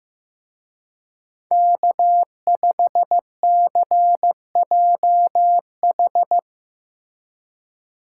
Morse code, heard as a keyed tone, about 17 wpm, 700 Hz. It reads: K5CJH
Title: Speed Racing - USA 1x3 Call Signs 15wpm